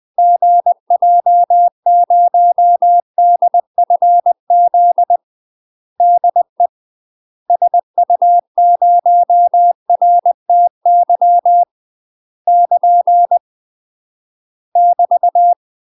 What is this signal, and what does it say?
Morse code, heard as a keyed tone, about 20 wpm, 700 Hz. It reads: GJ0DFZ DE SU0RTY <KN> <BT>
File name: UK MASTER QSOs TRAINING / 20WPM / UK MASTER QSOs 0066